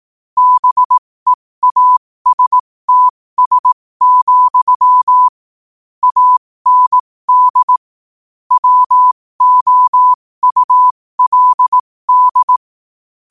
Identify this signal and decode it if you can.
Morse code, heard as a keyed tone, about 18 wpm, spaced as for 15 wpm, 1000 Hz. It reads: BEASTS, AND WOULD
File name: qotdfiles / 15wpm